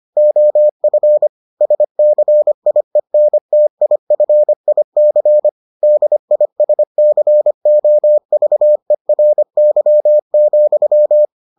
Morse code, heard as a keyed tone, about 25 wpm, 600 Hz. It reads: OF SCIENTIFIC DISCOVERY,